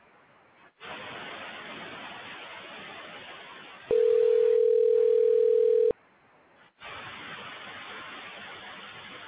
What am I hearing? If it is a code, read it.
Ringback tone (North America), 440 + 480 Hz.